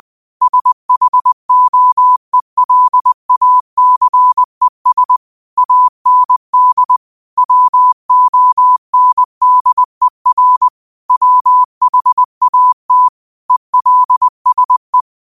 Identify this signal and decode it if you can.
Morse code, heard as a keyed tone, about 20 wpm, 1000 Hz. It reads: SHOELACES AND WONDER WHAT ELSE
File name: qotdfiles / 20wpm